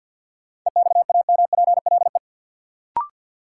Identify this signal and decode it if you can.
Morse code, heard as a keyed tone, about 52 wpm, 700 Hz. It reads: EXAMPLE
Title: Sets of 2 Words - New Word Drill - 50wpm